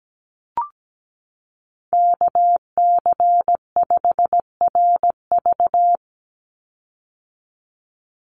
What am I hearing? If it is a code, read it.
Morse code, heard as a keyed tone, about 17 wpm, 700 Hz. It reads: KC5RV
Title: USA Call Signs 17wpm